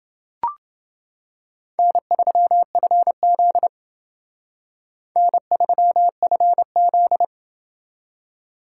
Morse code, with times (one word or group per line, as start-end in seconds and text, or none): 1.8-3.7 N3FZ
5.2-7.2 N3FZ